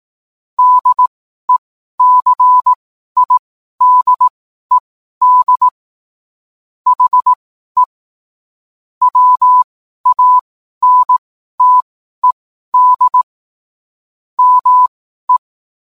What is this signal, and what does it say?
Morse code, heard as a keyed tone, about 18 wpm, spaced as for 12 wpm, 1000 Hz. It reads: DECIDED HE WANTED ME